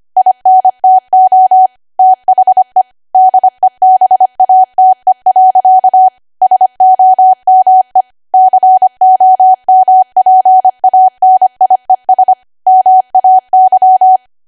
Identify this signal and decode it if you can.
Morse code, heard as a keyed tone, about 25 wpm, 750 Hz. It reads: INTO THE DEBATE. SOME COMPANIES MAY
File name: W1AW-Level3/140218 25WPM